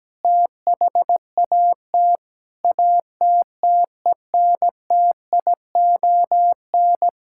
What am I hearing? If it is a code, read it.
Morse code, heard as a keyed tone, about 17 wpm, 700 Hz. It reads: THAT ATTENTION
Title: Sentences from Top 1000 Words - Review Part 6 - 17wpm